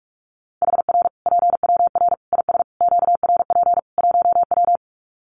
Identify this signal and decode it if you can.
Morse code, heard as a keyed tone, about 44 wpm, 700 Hz. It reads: HR PWR IS QRP 1W
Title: Common QSO Phrases 45wpm